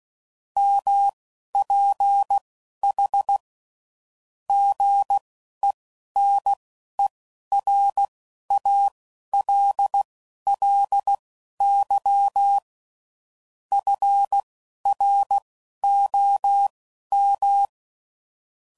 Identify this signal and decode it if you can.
Morse code, heard as a keyed tone, about 16 wpm, spaced as for 11 wpm, 800 Hz. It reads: MPH GENERALLY FROM
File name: CW Code Groups @10WPM / Weather Sets / Weather 4 10